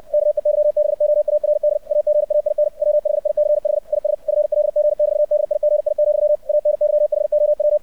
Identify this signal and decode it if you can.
Morse code, heard as a keyed tone, about 64 wpm, 600 Hz. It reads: KEYBOARD AGAIN WHICH IS FLG<BT>DIGI, ANYHOW
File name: archived / CRSnet